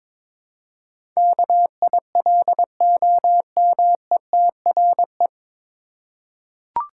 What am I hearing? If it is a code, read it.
Morse code, heard as a keyed tone, about 22 wpm, 700 Hz. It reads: KILOMETRE
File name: 9 Letter Words - First 3 Letters Given - 22wpm